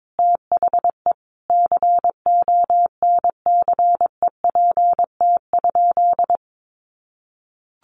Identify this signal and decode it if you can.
Morse code, heard as a keyed tone, about 22 wpm, 700 Hz. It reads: THE CONCEPT?
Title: Sentences from Top 1100 Words 22wpm